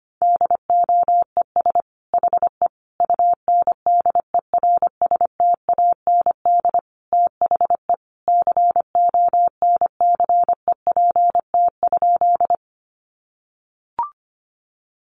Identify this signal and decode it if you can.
Morse code, heard as a keyed tone, about 25 wpm, 700 Hz. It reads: DOES HE UNDERSTAND THE CONCEPT?
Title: Sentences from Top 1100 Words 25wpm